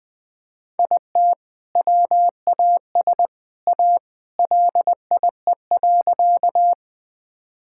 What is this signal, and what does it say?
Morse code, heard as a keyed tone, about 20 wpm, 700 Hz. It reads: IT WAS A LIE.
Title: Sentences from Top 1600 Words 20wpm